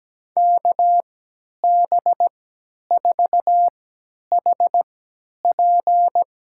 Morse code, heard as a keyed tone, about 17 wpm, spaced as for 10 wpm, 700 Hz. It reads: KB4HP
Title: USA Call Signs - 3x Character Spacing - 17wpm